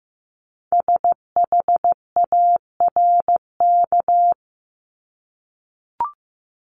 Morse code, heard as a keyed tone, about 15 wpm, 700 Hz. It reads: SHARK
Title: Multi-letter Phonemes Repeated 3x 15wpm